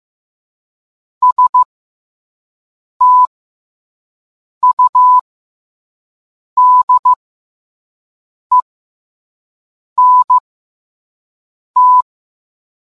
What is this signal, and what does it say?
Morse code, heard as a keyed tone, about 15 wpm, spaced as for 5 wpm, 1000 Hz. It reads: STUDENT